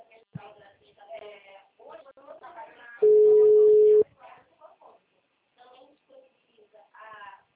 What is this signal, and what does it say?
Ringback tone (Europe, 425 Hz), with speech.